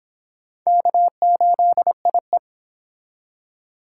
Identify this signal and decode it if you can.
Morse code, heard as a keyed tone, about 25 wpm, 700 Hz. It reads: K8IE